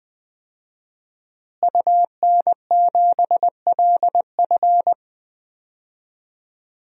Morse code, heard as a keyed tone, about 20 wpm, 700 Hz. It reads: UN7LF
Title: DXCC-100 Call Signs - 5 Characters - Repeated 3x - 20wpm